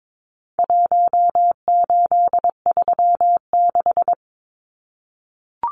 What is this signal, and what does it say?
Morse code, heard as a keyed tone, about 22 wpm, 700 Hz. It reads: 1836
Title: Numbers - Contest Style - 4-Digit 22wpm